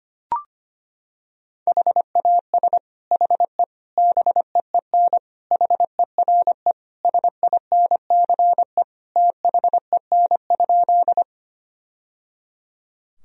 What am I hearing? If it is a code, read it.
Morse code, heard as a keyed tone, about 25 wpm, 700 Hz. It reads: HAS HE BEEN HERE SINCE THEN?
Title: Sentences from Top 1000 Words - Review Part 1 - 25wpm